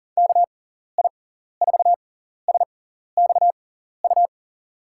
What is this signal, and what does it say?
Morse code, heard as a keyed tone, about 39 wpm, spaced as for 14 wpm, 700 Hz. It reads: KI4SXU
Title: USA Call Signs - 6 Characters - 6x Character Spacing - 40wpm